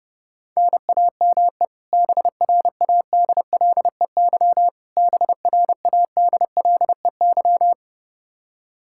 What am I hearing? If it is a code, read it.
Morse code, heard as a keyed tone, about 30 wpm, 700 Hz. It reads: NAME BRADLEY BRADLEY
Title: Practice QSO - Exchange 1 - Beginning - 30wpm